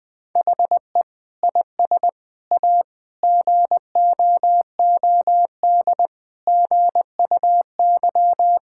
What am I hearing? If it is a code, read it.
Morse code, heard as a keyed tone, about 21 wpm, 700 Hz. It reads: HE IS A GOOD GUY
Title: Sets of 8 Words - 20wpm